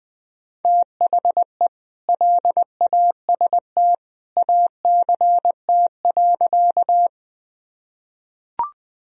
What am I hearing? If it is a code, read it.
Morse code, heard as a keyed tone, about 21 wpm, 700 Hz. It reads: THE LAST ACT.
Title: Sentences from Top 1000 Words - Review Part 1 - 20wpm